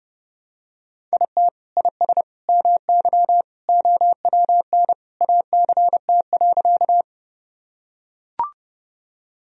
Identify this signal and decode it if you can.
Morse code, heard as a keyed tone, about 30 wpm, 700 Hz. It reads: IT IS MY OWN ACT.
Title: Sentences from Top 1000 Words - Review Part 1 - 30wpm